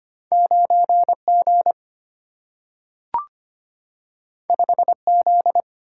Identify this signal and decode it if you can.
Morse code, heard as a keyed tone, about 25 wpm, 700 Hz. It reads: 9G 5Z
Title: Single-Letter Single-Number 25wpm